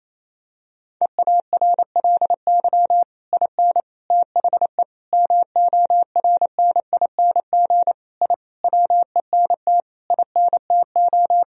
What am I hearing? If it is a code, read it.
Morse code, heard as a keyed tone, about 28 wpm, 700 Hz. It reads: EARLY IN THE MORNING I WENT INTO
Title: Sentences from Top 1000 Words - Review Part 2 - 28wpm